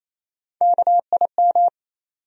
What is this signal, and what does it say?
Morse code, heard as a keyed tone, about 28 wpm, 700 Hz. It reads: KIM
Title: USA Common Names - Max 5 Letters - 28wpm